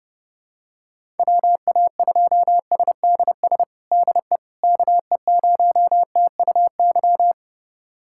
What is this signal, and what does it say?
Morse code, heard as a keyed tone, about 32 wpm, 700 Hz. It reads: WA2SDS DE KE0TUY
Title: USA Callsign DE Callsign - 6 Characters - 30wpm